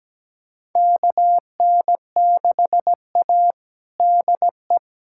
Morse code, heard as a keyed tone, about 17 wpm, 700 Hz. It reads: KN6A DE
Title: USA Callsign DE Callsign - 4 Characters - 17wpm